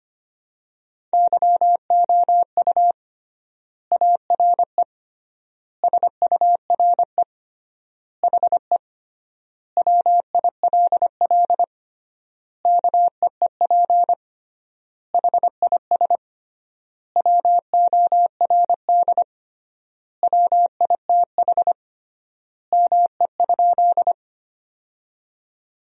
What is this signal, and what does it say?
Morse code, heard as a keyed tone, about 25 wpm, 700 Hz. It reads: YOU ARE SURE HE WILL KEEP HIS WORD WITH ME?
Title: 3x Word-Spacing - Sentences from Top 300 Words - 25wpm